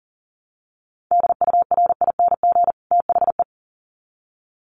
Morse code, heard as a keyed tone, about 40 wpm, 700 Hz. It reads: DURING THE
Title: Sets of 2 Words 40wpm